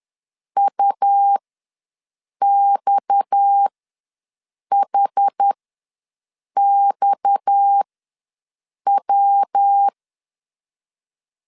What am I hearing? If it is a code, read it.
Morse code, heard as a keyed tone, about 10 wpm, spaced as for 6 wpm, 800 Hz. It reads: UXHXW
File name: LEZIONI / 6 PKX0 / 6 PKX ws 10wpm 3706